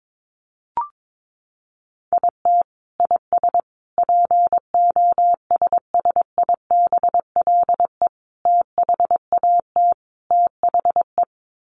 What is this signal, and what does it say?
Morse code, heard as a keyed tone, about 22 wpm, 700 Hz. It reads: IT IS POSSIBLE THAT THE
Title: Sets of 5 Words Encore2 22wpm